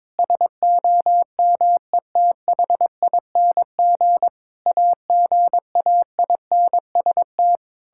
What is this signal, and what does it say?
Morse code, heard as a keyed tone, about 22 wpm, 700 Hz. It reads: SOMETHING AGAINST
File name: Sentences with 9-Word Phrases - 22wpm